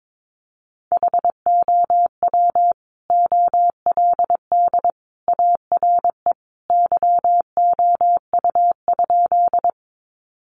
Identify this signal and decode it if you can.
Morse code, heard as a keyed tone, about 22 wpm, 700 Hz. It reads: HOW OLD ARE YOU?
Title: Sentences from Top 200 Words 22wpm